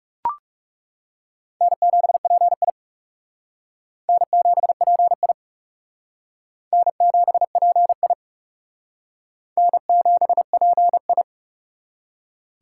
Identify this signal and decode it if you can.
Morse code, heard as a keyed tone, about 38 wpm, 700 Hz. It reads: N7PI N7PI N7PI N7PI